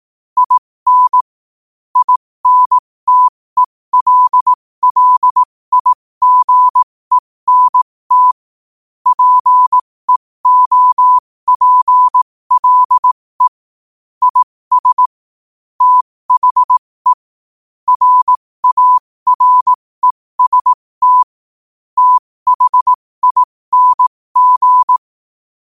Morse code, heard as a keyed tone, about 18 wpm, spaced as for 15 wpm, 1000 Hz. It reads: IN INTELLIGENT PEOPLE IS THE RAREST THING
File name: qotdfiles / 15wpm